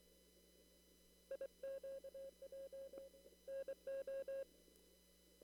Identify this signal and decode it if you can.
Morse code, heard as a keyed tone, about 23 wpm, 540 Hz. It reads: IQJ NO
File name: CW / kt5j wpx cw-op-k5pi